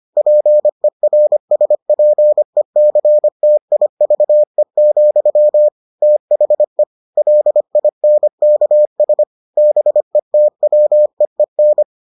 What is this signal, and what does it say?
Morse code, heard as a keyed tone, about 25 wpm, 600 Hz. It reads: PERSPECTIVE, THE LINKS BETWEEN